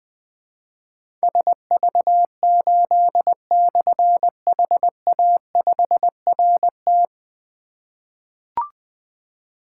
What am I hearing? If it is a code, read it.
Morse code, heard as a keyed tone, about 20 wpm, 700 Hz. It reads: SV8/HA5RT
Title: Difficult Call Signs 20wpm